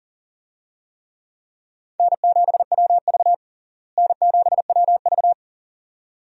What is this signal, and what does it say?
Morse code, heard as a keyed tone, about 40 wpm, 700 Hz. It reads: N7WV N7WV
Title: USA Call Signs - 4 Characters - Repeated 2x - 40wpm